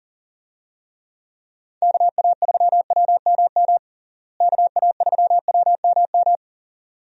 Morse code, heard as a keyed tone, about 40 wpm, 700 Hz. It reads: KA3WMM KA3WMM
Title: USA Call Signs - 6 Characters - Repeated 2x - 40wpm